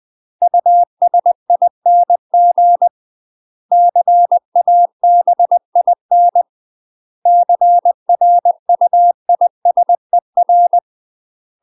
Morse code, heard as a keyed tone, about 20 wpm, 700 Hz. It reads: USING CABIN CRUISER
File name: UK MASTER QSOs TRAINING / 20WPM / UK MASTER QSOs 0050